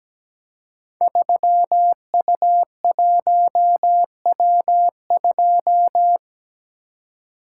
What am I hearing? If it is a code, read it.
Morse code, heard as a keyed tone, about 17 wpm, 700 Hz. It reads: 3U1W2
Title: Code Groups - 5 Letter:Numbers - 17wpm